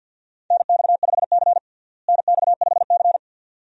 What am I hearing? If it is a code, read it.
Morse code, heard as a keyed tone, about 46 wpm, 700 Hz. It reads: NX5C NX5C
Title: USA Call Signs - 4 Characters - Repeated 2x - 50wpm